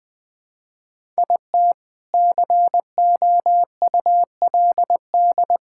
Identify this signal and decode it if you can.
Morse code, heard as a keyed tone, about 20 wpm, 700 Hz. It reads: IT COULD
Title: Build A Phrase - 3-Word Phrases - 20wpm